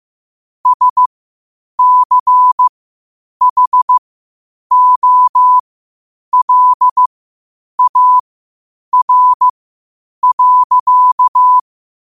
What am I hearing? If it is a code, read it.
Morse code, heard as a keyed tone, about 15 wpm, spaced as for 8 wpm, 1000 Hz. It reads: SCHOLAR.
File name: 8wpm